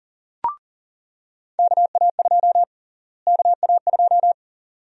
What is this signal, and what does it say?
Morse code, heard as a keyed tone, about 40 wpm, 700 Hz. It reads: KA2 KA2
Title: USA Call Sign Prefixes - Repeated 2x - 40wpm